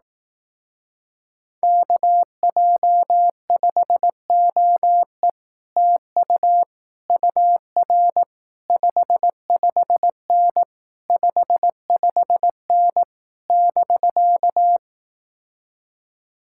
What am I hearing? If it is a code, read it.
Morse code, heard as a keyed tone, about 18 wpm, 700 Hz. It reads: KJ5OE TU UR 55N 55N <BK>
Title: Speed-Racing Common QSO Phrases Encore - 15wpm